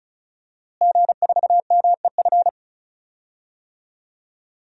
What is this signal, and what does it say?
Morse code, heard as a keyed tone, about 36 wpm, 700 Hz. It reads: G4MEF
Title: DXCC-100 Call Signs - 5 Characters - Repeated 2x - 35wpm